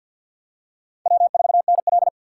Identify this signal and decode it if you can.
Morse code, heard as a keyed tone, about 52 wpm, 700 Hz. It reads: W4NL
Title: USA Call Signs - 4 Characters - 6x Character Spacing - 50wpm